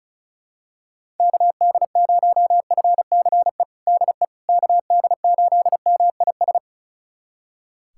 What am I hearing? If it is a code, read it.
Morse code, heard as a keyed tone, about 36 wpm, 700 Hz. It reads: KD0FCE DE KD8MIS
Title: USA Callsign DE Callsign 35wpm